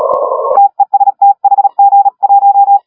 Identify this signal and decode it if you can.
Morse code, heard as a keyed tone, about 36 wpm, 810 Hz. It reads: TEST HG1
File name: HG1R CW SSB MIX